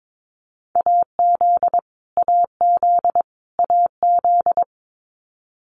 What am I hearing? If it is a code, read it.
Morse code, heard as a keyed tone, about 22 wpm, 700 Hz. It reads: AZ AZ AZ
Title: US States Repeated 3x 22wpm